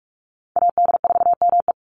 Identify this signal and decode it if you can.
Morse code, heard as a keyed tone, about 46 wpm, 690 Hz. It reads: AD4ME
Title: USA Call Signs 45wpm